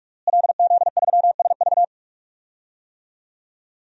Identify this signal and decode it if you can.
Morse code, heard as a keyed tone, about 47 wpm, 700 Hz. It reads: LZ3SV
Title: DXCC-100 Call Signs - 5 Characters - Repeated 2x - 45wpm